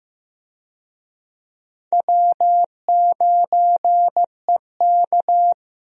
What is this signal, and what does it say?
Morse code, heard as a keyed tone, about 16 wpm, 700 Hz. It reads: W9EK
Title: USA Call Signs - 4 Characters - Repeated 2x - 15wpm